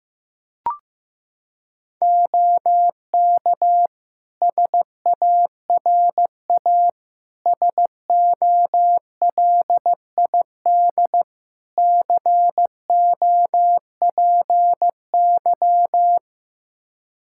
Morse code, heard as a keyed tone, about 15 wpm, 700 Hz. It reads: OK SARA SOLID COPY
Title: Practice QSO - Exchange 3 - More Info - 15wpm